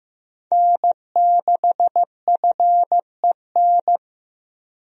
Morse code, heard as a keyed tone, about 15 wpm, 700 Hz. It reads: N6FEN
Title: USA Call Signs - 5 Characters - Repeated 3x - 15wpm